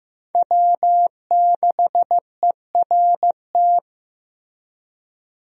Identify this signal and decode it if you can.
Morse code, heard as a keyed tone, about 15 wpm, 700 Hz. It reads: W6ERT
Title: USA Call Signs - 5 Characters - Repeated 2x - 15wpm